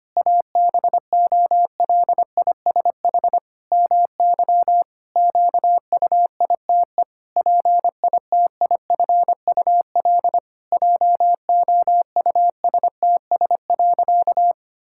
Morse code, heard as a keyed tone, about 25 wpm, 700 Hz. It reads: ABOLISH MY QUITE PITIFUL JOUSTS.
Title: Pangram Challenge v2 - 25wpm